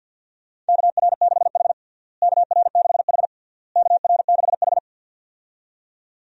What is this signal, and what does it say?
Morse code, heard as a keyed tone, about 53 wpm, 700 Hz. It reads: KR6H KR6H KR6H
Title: USA Call Signs - 4 Characters - Repeated 3x - 50wpm